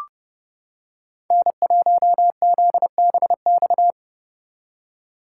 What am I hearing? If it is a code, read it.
Morse code, heard as a keyed tone, about 29 wpm, 700 Hz. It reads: N1ZBX